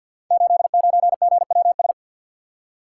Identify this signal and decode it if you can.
Morse code, heard as a keyed tone, about 51 wpm, 700 Hz. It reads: 89GWS
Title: Code Groups - 5 Letter:Numbers - 50wpm